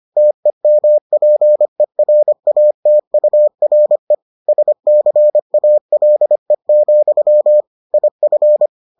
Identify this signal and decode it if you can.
Morse code, heard as a keyed tone, about 25 wpm, 600 Hz. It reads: TEMPERATURE SCALE, IF